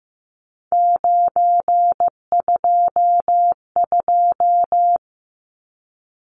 Morse code, heard as a keyed tone, about 15 wpm, 700 Hz. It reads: 922
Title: Numbers - 3-Digit 15wpm